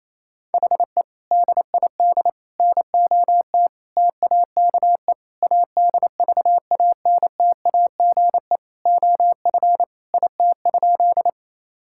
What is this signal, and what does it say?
Morse code, heard as a keyed tone, about 28 wpm, 700 Hz. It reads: HE DID NOT TAKE ADVANTAGE OF IT?
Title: Sentences from Top 1000 Words - Review Part 6 - 28wpm